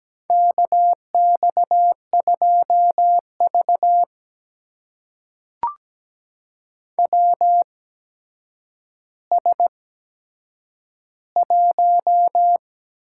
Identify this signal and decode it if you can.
Morse code, heard as a keyed tone, about 17 wpm, spaced as for 8 wpm, 700 Hz. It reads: KX2V WS1
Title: USA Call Signs - 4 Characters - 8x Character Spacing - 17wpm